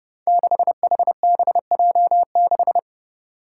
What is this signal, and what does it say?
Morse code, heard as a keyed tone, about 30 wpm, 700 Hz. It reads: 6HBJ6
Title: Five Challenging Letter-Number Combinations 30wpm